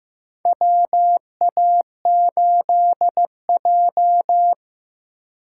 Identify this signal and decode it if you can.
Morse code, heard as a keyed tone, about 15 wpm, 700 Hz. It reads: WA8J
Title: USA Call Signs - 4 Characters - 2x Character Spacing - 15wpm